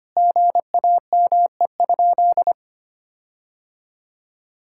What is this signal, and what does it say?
Morse code, heard as a keyed tone, about 25 wpm, 700 Hz. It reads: GAME?
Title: Sentences from Top 1600 Words 25wpm